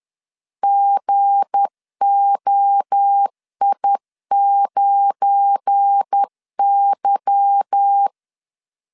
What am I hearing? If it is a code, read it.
Morse code, heard as a keyed tone, about 10 wpm, 800 Hz. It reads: GOI9Y
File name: CW / DET / 1 9 1500C S B 10wpm 4439